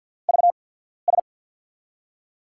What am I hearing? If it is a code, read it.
Morse code, heard as a keyed tone, about 53 wpm, spaced as for 14 wpm, 700 Hz. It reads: VS
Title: USA Call Signs - 5 Characters - 8x Character Spacing - 50wpm